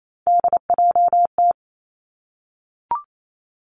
Morse code, heard as a keyed tone, about 28 wpm, 700 Hz. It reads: DJT
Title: Three Challenging Letter-Number Combinations 28wpm